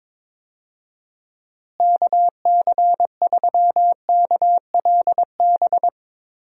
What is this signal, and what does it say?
Morse code, heard as a keyed tone, about 22 wpm, 700 Hz. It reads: KC3KLB
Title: USA Call Signs - 6 Characters - Repeated 3x - 22wpm